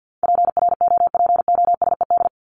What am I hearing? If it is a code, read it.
Morse code, heard as a keyed tone, about 50 wpm, 700 Hz. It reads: PROPOSED
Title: ED Suffix - Top 100 Words with Base Words 50wpm